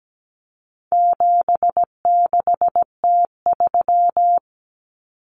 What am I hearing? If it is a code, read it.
Morse code, heard as a keyed tone, about 17 wpm, 700 Hz. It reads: 76T3
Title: Common Cut Numbers - 4-Digit 17wpm